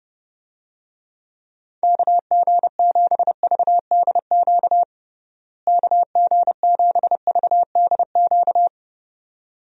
Morse code, heard as a keyed tone, about 30 wpm, 700 Hz. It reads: KG7VDQ KG7VDQ
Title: USA Call Signs - 6 Characters - Repeated 2x - 30wpm